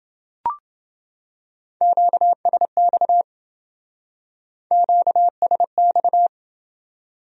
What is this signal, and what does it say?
Morse code, heard as a keyed tone, about 29 wpm, 700 Hz. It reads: QSX QSX